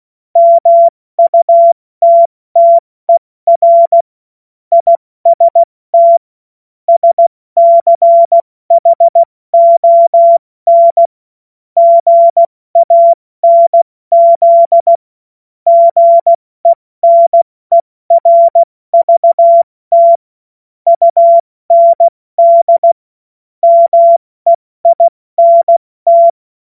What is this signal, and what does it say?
Morse code, heard as a keyed tone, about 16 wpm, 680 Hz. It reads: MUTTER IST SCHON GANZ GENERVT UND MEINT
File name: Morsekurs / Witze / Witz 21. März 2023 70 F80